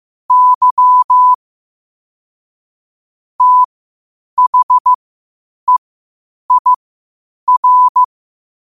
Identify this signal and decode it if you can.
Morse code, heard as a keyed tone, about 15 wpm, spaced as for 8 wpm, 1000 Hz. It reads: Y THEIR
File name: qotdfiles / 8wpm